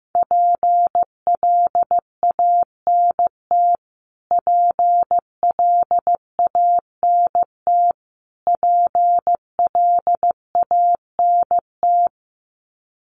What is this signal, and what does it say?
Morse code, heard as a keyed tone, about 15 wpm, 700 Hz. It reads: PLANT PLANT PLANT
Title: Top 500 Words Repeated 15wpm